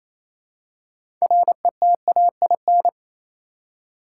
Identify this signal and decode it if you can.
Morse code, heard as a keyed tone, about 28 wpm, 700 Hz. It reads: RETAIN
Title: Six Letter Words - 28wpm